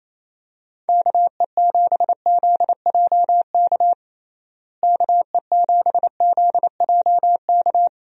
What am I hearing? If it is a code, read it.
Morse code, heard as a keyed tone, about 28 wpm, 700 Hz. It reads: KE7ZJK KE7ZJK
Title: USA Call Signs - 6 Characters - Repeated 2x - 28wpm